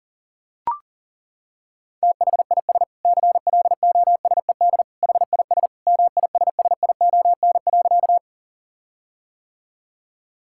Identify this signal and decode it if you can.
Morse code, heard as a keyed tone, about 40 wpm, 700 Hz. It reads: THIS CLOSED HIS MISSION.
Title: Sentences from Top 1600 Words 40wpm